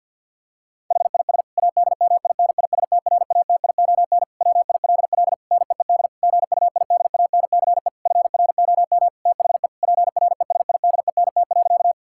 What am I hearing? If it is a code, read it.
Morse code, heard as a keyed tone, about 50 wpm, 700 Hz. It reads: HIS ADMINISTRATION WILL NEED GUIDANCE FROM THE PRESIDENT.
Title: Sentences from Top 1600 Words 50wpm